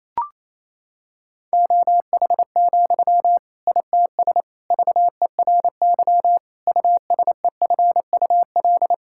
Morse code, heard as a keyed tone, about 28 wpm, 700 Hz. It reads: OH, ITS VERY USEFUL
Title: Sentences from Top 1000 Words - Review Part 6 - 28wpm